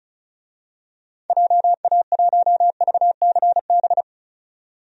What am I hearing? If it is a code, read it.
Morse code, heard as a keyed tone, about 34 wpm, 700 Hz. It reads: JA1VCB
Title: DXCC-100 Call Signs - 6 Characters - 35wpm